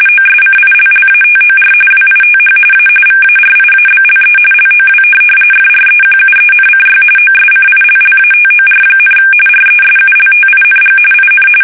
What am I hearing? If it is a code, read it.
RTTY, about 45.45 baud. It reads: OSCILLATOR WITH A VARIABLE LOW FREQUENCY OSCILLATOR, RESULTING IN
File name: fsk850 / teletype-RTTY